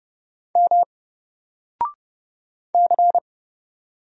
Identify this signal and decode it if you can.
Morse code, heard as a keyed tone, about 30 wpm, 700 Hz. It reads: M C
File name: Single-Letter-Number-30wpm